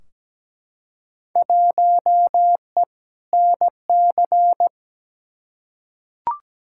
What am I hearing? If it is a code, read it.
Morse code, heard as a keyed tone, about 17 wpm, 700 Hz. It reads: 1E NC
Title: Field Day Exchanges - 17wpm